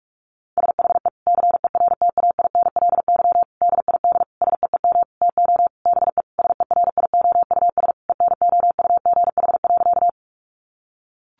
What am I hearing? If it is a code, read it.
Morse code, heard as a keyed tone, about 44 wpm, 700 Hz. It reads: SHE CERTAINLY DID SEEM TO BE SERIOUS ENOUGH.
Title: Sentences from Top 1100 Words 45wpm